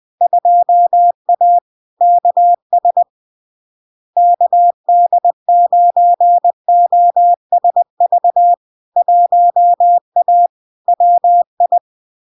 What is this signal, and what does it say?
Morse code, heard as a keyed tone, about 20 wpm, 700 Hz. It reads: 2A KS KD9OSV 1A WI
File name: LARK-FD-2022-Log-in-CW-20wpm-700-Hz